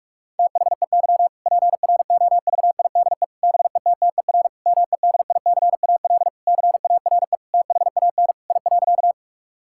Morse code, heard as a keyed tone, about 42 wpm, 700 Hz. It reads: THEY PROVIDE BETTER MEDICAL CARE THAN I.
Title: Sentences from Top 1000 Words - Review Part 5 - 45wpm